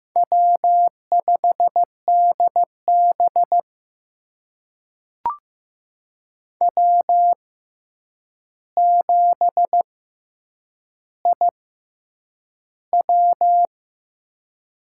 Morse code, heard as a keyed tone, about 15 wpm, spaced as for 8 wpm, 700 Hz. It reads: W5DB W7IW
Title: USA Call Signs - 4 Characters - 6x Character Spacing - 15wpm